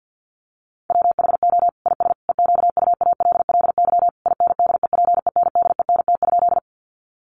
Morse code, heard as a keyed tone, about 50 wpm, 700 Hz. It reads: WHO IS EQUALLY INDEPENDENT?
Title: Sentences from Top 1400 Words 50wpm